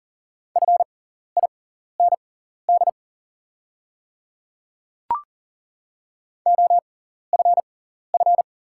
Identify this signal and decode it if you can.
Morse code, heard as a keyed tone, about 40 wpm, spaced as for 14 wpm, 690 Hz. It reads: FIND OFF
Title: 6x Character Spacing - Top 300 Words - 40wpm